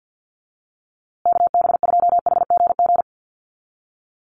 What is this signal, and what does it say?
Morse code, heard as a keyed tone, about 50 wpm, 700 Hz. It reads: KB2HGG
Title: USA Call Signs Repeated 2x 50wpm